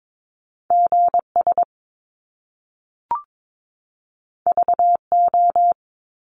Morse code, heard as a keyed tone, about 22 wpm, 700 Hz. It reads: GS VO
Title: Top Two Letter Combinations in Words 22wpm